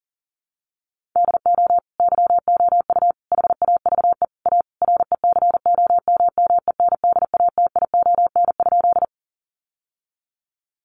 Morse code, heard as a keyed tone, about 40 wpm, 700 Hz. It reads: DO YOU HAVE A RECOMMENDATION?
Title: Sentences from Top 1400 Words 40wpm